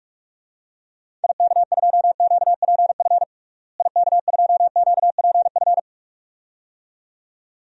Morse code, heard as a keyed tone, about 41 wpm, 700 Hz. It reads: IK2QPF IK2QPF
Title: DXCC-100 Call Signs - 6 Characters - Repeated 2x - 45wpm